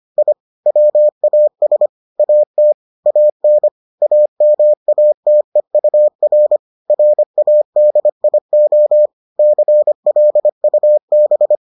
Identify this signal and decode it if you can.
Morse code, heard as a keyed tone, about 25 wpm, 600 Hz. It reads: I WAS AT AN AMATEUR RADIO CLUB